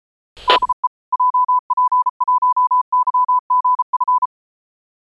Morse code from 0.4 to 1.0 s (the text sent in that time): DE